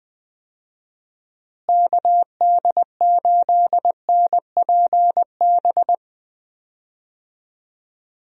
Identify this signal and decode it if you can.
Morse code, heard as a keyed tone, about 21 wpm, 700 Hz. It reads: KD8NPB
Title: USA Call Signs - 6 Characters - Repeated 3x - 20wpm